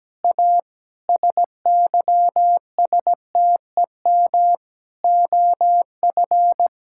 Morse code, heard as a keyed tone, about 17 wpm, 700 Hz. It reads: A SYSTEM OF
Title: Sentences from Top 1000 Words - Review Part 6 - 17wpm